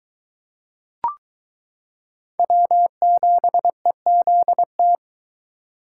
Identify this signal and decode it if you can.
Morse code, heard as a keyed tone, about 21 wpm, 700 Hz. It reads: W7EZT